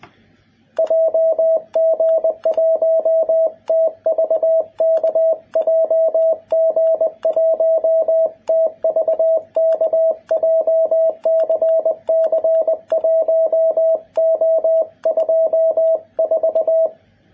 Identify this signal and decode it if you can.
Morse code, heard as a keyed tone, about 20 wpm, 650 Hz, with radio noise. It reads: JG1TVK JG1TVK JCC1O24
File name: jh8cbh / 2021cw